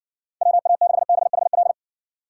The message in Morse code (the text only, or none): WA6BHL